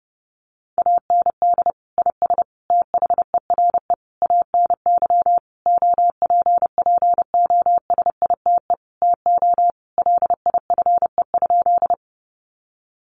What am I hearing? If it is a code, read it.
Morse code, heard as a keyed tone, about 30 wpm, 700 Hz. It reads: AND IS THERE ANY OPPOSITE TO LIFE?
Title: Sentences from Top 1300 Words 30wpm